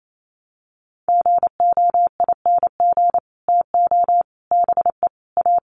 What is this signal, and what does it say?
Morse code, heard as a keyed tone, about 28 wpm, 700 Hz. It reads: GOING TO BE A
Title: Sets of 5 Words Encore2 28wpm